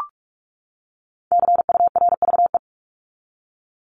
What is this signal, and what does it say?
Morse code, heard as a keyed tone, about 44 wpm, 700 Hz. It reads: CURVE
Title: Top 1501-1750 Words 45wpm